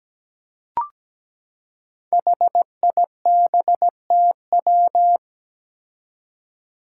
Morse code, heard as a keyed tone, about 17 wpm, 700 Hz. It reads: HIBTW
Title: Five Challenging Letter-Number Combinations 17wpm